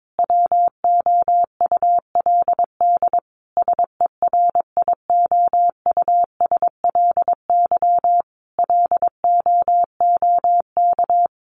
Morse code, heard as a keyed tone, about 22 wpm, 700 Hz. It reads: WOULD SERIOUSLY LOOK
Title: Sentences from Top 1200 Words 22wpm